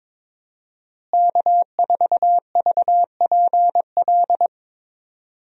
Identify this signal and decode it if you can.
Morse code, heard as a keyed tone, about 21 wpm, 700 Hz. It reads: K4VPL
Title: USA Call Signs - 5 Characters - Repeated 3x - 22wpm